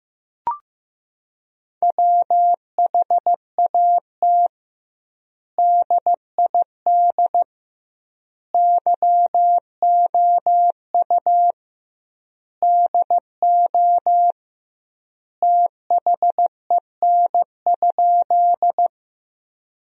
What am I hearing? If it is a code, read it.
Morse code, heard as a keyed tone, about 16 wpm, 700 Hz. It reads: WHAT DID YOU DO THEN?
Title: 2x Word-Spacing - Sentences from Top 100 Words - 15wpm